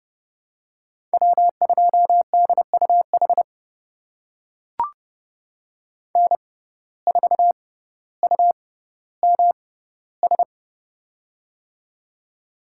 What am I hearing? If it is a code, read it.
Morse code, heard as a keyed tone, about 31 wpm, spaced as for 16 wpm, 700 Hz. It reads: W2DUH N4UMS
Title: USA Call Signs - 5 Characters - 6x Character Spacing - 30wpm